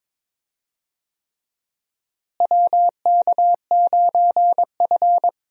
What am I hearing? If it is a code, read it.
Morse code, heard as a keyed tone, about 22 wpm, 700 Hz. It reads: WK9F
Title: USA Call Signs - 4 Characters - Repeated 3x - 22wpm